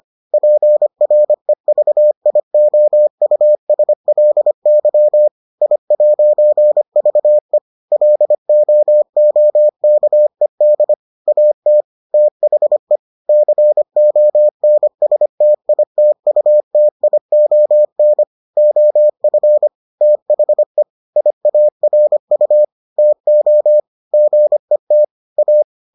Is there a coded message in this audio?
Yes: Morse code from 0.3 to 25.6 s, heard as a keyed tone, about 25 wpm, 600 Hz. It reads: PREVIOUSLY I'VE LOOKED AT THE CONSTITUTION OF THE IARU TO GET A